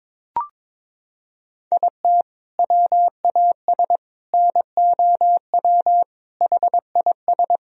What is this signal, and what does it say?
Morse code, heard as a keyed tone, about 22 wpm, 700 Hz. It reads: IT WAS NOW HIS
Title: Sentences from Top 200 Words and Abbreviations 22wpm